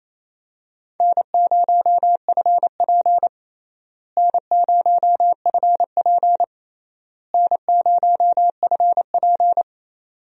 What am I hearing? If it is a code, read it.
Morse code, heard as a keyed tone, about 28 wpm, 700 Hz. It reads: N0FP N0FP N0FP
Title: USA Call Signs - 4 Characters - Repeated 3x - 28wpm